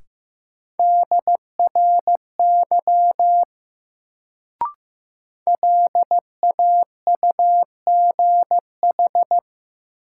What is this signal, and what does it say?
Morse code, heard as a keyed tone, about 15 wpm, 700 Hz. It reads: DRY LAUGH
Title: Sentences from Top 500 Words - New Word Drill - 15wpm